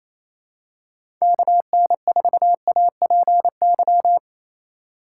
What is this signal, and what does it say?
Morse code, heard as a keyed tone, about 28 wpm, 700 Hz. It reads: KN4APY
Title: USA Call Signs - 6 Characters - 3x Character Spacing - 28wpm